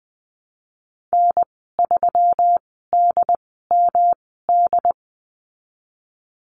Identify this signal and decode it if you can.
Morse code, heard as a keyed tone, about 20 wpm, spaced as for 15 wpm, 700 Hz. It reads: N3DMD
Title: 213 Lesson 34g - Number 8 - Call Signs 20wpm